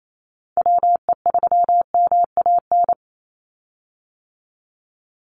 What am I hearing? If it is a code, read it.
Morse code, heard as a keyed tone, about 28 wpm, 700 Hz. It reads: WE3MAN
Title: USA Call Signs 28wpm